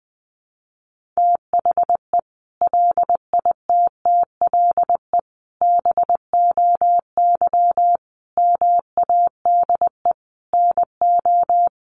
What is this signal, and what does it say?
Morse code, heard as a keyed tone, about 20 wpm, 700 Hz. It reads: THE LITTLE BOY MADE NO
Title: Sentences from Top 1400 Words 20wpm